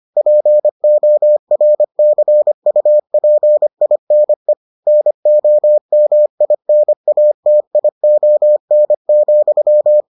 Morse code, heard as a keyed tone, about 25 wpm, 600 Hz. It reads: PORCUPINE NOMINATION,